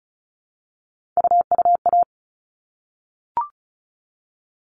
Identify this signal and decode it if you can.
Morse code, heard as a keyed tone, about 34 wpm, 700 Hz. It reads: UUA
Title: All Cut Numbers - 3-Digit 35wpm